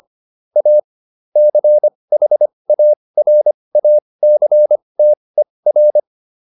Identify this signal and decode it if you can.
Morse code, heard as a keyed tone, about 25 wpm, spaced as for 20 wpm, 600 Hz. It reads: A CHARACTER
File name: SS315 20